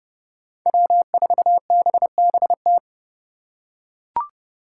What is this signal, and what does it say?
Morse code, heard as a keyed tone, about 30 wpm, 700 Hz. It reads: W4BBT
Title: K1USN SST QSO Practice - 30wpm